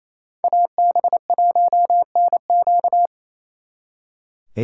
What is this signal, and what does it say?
Morse code, heard as a keyed tone, about 33 wpm, 700 Hz. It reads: AB1NQ
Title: Speed Racing - USA 2x2 Call Signs 28wpm